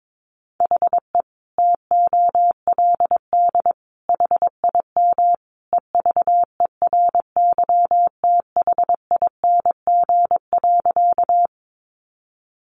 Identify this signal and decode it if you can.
Morse code, heard as a keyed tone, about 22 wpm, 700 Hz. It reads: HE TOLD HIM EVERYTHING.
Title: Sentences from Top 700 Words 22wpm